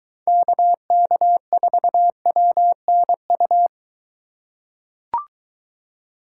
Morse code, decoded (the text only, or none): KK4WNU